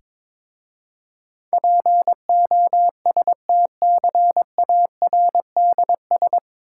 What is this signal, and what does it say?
Morse code, heard as a keyed tone, about 22 wpm, 690 Hz. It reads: POSTCARDS
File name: Difficult Words with Hints - 22wpm